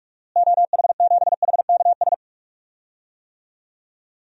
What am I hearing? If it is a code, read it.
Morse code, heard as a keyed tone, about 46 wpm, 700 Hz. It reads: OH7HKS
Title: DXCC-100 Call Signs - 6 Characters - Repeated 3x - 45wpm